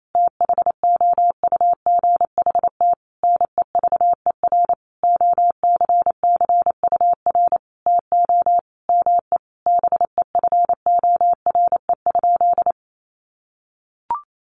Morse code, heard as a keyed tone, about 28 wpm, 700 Hz. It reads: THOUGHT NEVER OCCUR TO ME BEFORE?
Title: Sentences from Top 900 Words 28wpm